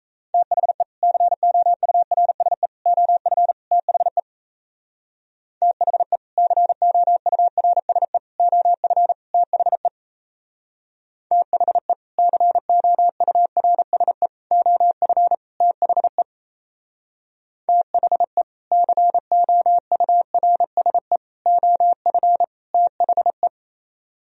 Morse code, heard as a keyed tone, about 35 wpm, 700 Hz. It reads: THE COURSE OF THE THE COURSE OF THE THE COURSE OF THE THE COURSE OF THE
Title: Speed-Racing - Sets of 4 Words 28wpm